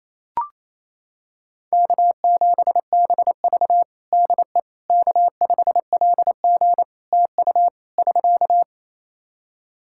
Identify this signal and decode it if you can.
Morse code, heard as a keyed tone, about 28 wpm, 700 Hz. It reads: K7BV DE K5LG TU <SK>
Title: K1USN SST QSO Practice 28wpm